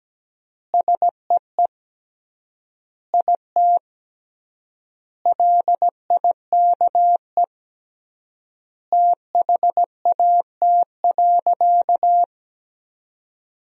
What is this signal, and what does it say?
Morse code, heard as a keyed tone, about 17 wpm, 700 Hz. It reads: SEE IT LIKE THAT.
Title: 3x Word-Spacing - Sentences from Top 300 Words - 17wpm